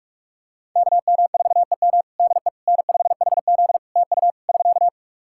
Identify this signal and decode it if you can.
Morse code, heard as a keyed tone, about 46 wpm, 700 Hz. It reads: KM4EM DE N5HZ TU <SK>
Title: Practice QSO - Exchange 4 - Ending - 45wpm